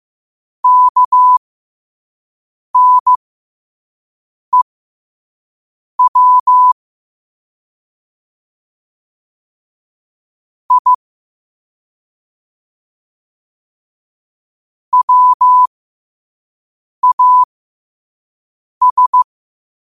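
Morse code, heard as a keyed tone, about 15 wpm, spaced as for 5 wpm, 1000 Hz. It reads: KNEW I WAS
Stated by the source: Morse code practice at 5 wpm.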